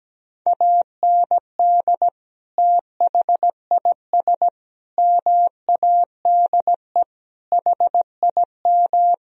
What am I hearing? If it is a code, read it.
Morse code, heard as a keyed tone, about 17 wpm, 700 Hz. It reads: AND THIS MADE HIM
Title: Sentences from Top 1000 Words - Review Part 1 - 17wpm